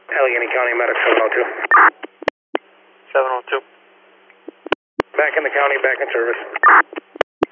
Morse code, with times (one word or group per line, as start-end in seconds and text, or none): none